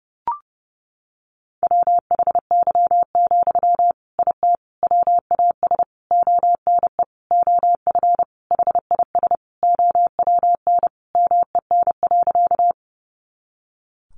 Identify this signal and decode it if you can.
Morse code, heard as a keyed tone, about 30 wpm, 700 Hz. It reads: WHY, IT WAS ONE OF HIS OWN MEN.
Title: Sentences from Top 300 Words 30wpm